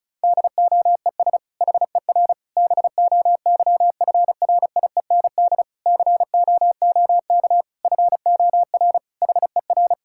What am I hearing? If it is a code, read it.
Morse code, heard as a keyed tone, about 36 wpm, 700 Hz. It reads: DOES HER BOYFRIEND COOK FOR HER
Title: Sentences from Top 1000 Words - Review Part 5 - 35wpm